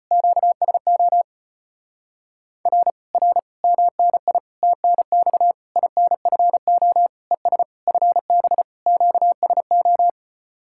Morse code, heard as a keyed tone, about 33 wpm, 700 Hz. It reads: QSO R R MNI TNX INFO ES FB QSO